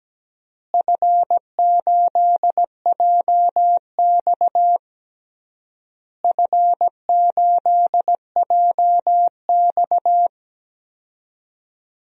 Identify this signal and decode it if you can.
Morse code, heard as a keyed tone, about 17 wpm, 700 Hz. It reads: F8JX F8JX
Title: DXCC-100 Call Signs - 4 Characters - Repeated 3x - 17wpm